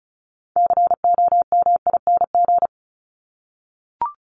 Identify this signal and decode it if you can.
Morse code, heard as a keyed tone, about 34 wpm, 700 Hz. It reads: COMING
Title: ING Suffix - Top 100 Words with Base Words 35wpm